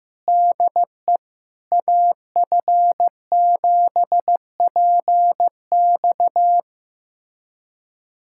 Morse code, heard as a keyed tone, about 15 wpm, 700 Hz. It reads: DE AF7PX
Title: USA Callsign DE Callsign - 5 Characters - 15wpm